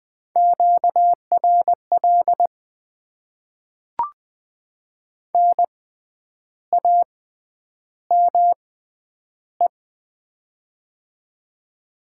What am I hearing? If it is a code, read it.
Morse code, heard as a keyed tone, about 20 wpm, spaced as for 9 wpm, 700 Hz. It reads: QRL NAME
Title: Top 20 QSO Elements - 6x Character Spacing - 20wpm